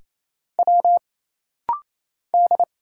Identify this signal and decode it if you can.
Morse code, heard as a keyed tone, about 27 wpm, 700 Hz. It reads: W D
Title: Single Letters - Rapid-Fire 28wpm